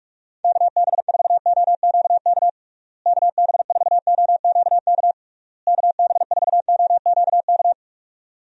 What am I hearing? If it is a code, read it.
Morse code, heard as a keyed tone, about 46 wpm, 700 Hz. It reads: KB4OQK KB4OQK KB4OQK
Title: USA Call Signs - 6 Characters - Repeated 3x - 45wpm